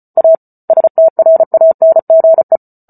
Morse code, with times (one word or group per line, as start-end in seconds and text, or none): 0.2-0.3 A
0.7-2.6 STRANGE